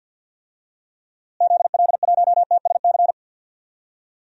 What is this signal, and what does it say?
Morse code, heard as a keyed tone, about 53 wpm, 700 Hz. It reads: ZL1TSC
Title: DXCC-100 Call Signs - 6 Characters - Repeated 3x - 50wpm